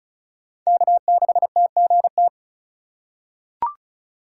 Morse code, heard as a keyed tone, about 36 wpm, 700 Hz. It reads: K6TGT
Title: USA Call Signs - 5 Characters - 2x Character Spacing - 35wpm